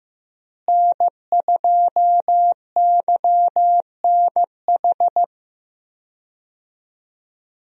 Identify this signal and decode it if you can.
Morse code, heard as a keyed tone, about 15 wpm, 700 Hz. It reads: N2YNH
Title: USA Call Signs - 5 Characters - Repeated 3x - 15wpm